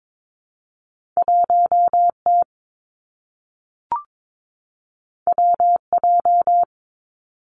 Morse code, heard as a keyed tone, about 22 wpm, 700 Hz. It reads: 1 T W J
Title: Two Challenging Letter-Number Combinations 22wpm